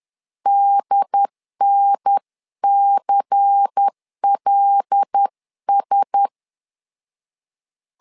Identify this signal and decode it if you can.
Morse code, heard as a keyed tone, about 10 wpm, 800 Hz. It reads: DNCLS
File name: LEZIONI / 5 LBVC / 5 LBVC 10wpm 2349